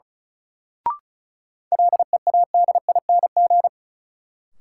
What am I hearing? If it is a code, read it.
Morse code, heard as a keyed tone, about 36 wpm, 700 Hz. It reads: LEADING
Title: Sentences from Top 1100 Words - New Word Drill - 35wpm